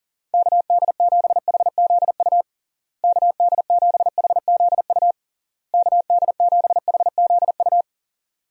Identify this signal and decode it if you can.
Morse code, heard as a keyed tone, about 40 wpm, 700 Hz. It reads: KD7HZU KD7HZU KD7HZU
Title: USA Call Signs - 6 Characters - Repeated 3x - 40wpm